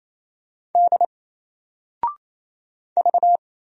Morse code, heard as a keyed tone, about 27 wpm, 700 Hz. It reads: D V
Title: Single Character (Letter:Number:Punctuation:Prosign) 28wpm